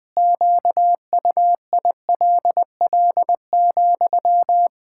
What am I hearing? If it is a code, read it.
Morse code, heard as a keyed tone, about 20 wpm, 700 Hz. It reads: QUILL,
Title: Pangram Challenge v2 - 20wpm